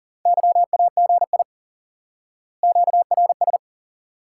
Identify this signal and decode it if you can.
Morse code, heard as a keyed tone, about 38 wpm, 700 Hz. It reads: YAGI QRS
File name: IQE Course 3 - Lesson 89 - YAGI - Two Elements - 40wpm